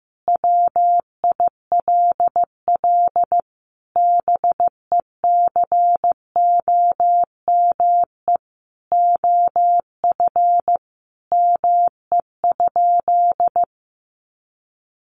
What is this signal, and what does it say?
Morse code, heard as a keyed tone, about 15 wpm, 700 Hz. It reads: WILL BECOME OF ME?
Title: Sentences from Top 600 Words 15wpm